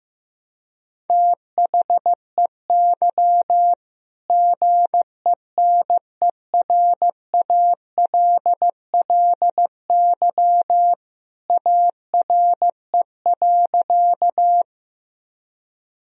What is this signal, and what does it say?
Morse code, heard as a keyed tone, about 16 wpm, 700 Hz. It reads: THEY GENERALLY ARE.
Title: Sentences from Top 1000 Words - Review Part 4 - 15wpm